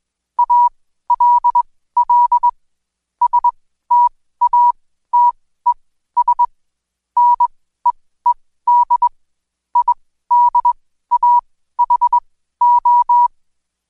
Morse code, heard as a keyed tone, about 21 wpm, spaced as for 14 wpm, 970 Hz. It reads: ALL STATES NEED IDAHO
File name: Code Quick / AudioDownld2 / Seg124-Day29-DayDrill 29